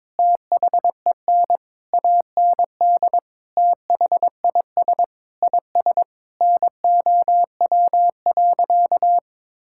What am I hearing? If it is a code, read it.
Morse code, heard as a keyed tone, about 22 wpm, 700 Hz. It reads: THEN AND THIS IS NOW.
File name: practice / 22wpm / Sentences with 8-Word Phrases - 22wpm